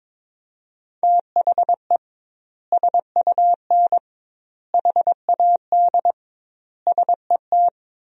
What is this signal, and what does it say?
Morse code, heard as a keyed tone, about 22 wpm, 700 Hz. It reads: THE SUN HAD SET
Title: 2x Word-Spacing - Sentences from Top 300 Words - 22wpm